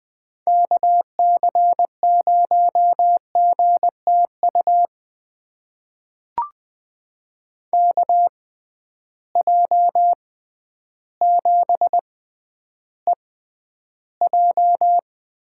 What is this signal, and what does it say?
Morse code, heard as a keyed tone, about 20 wpm, spaced as for 11 wpm, 700 Hz. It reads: KC0GTU KJ7EJ
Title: USA Call Signs - 6 Characters - 6x Character Spacing - 20wpm